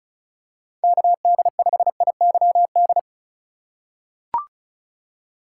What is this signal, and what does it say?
Morse code, heard as a keyed tone, about 36 wpm, 700 Hz. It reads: KD5IYD
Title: USA Call Signs - 6 Characters - 3x Character Spacing - 35wpm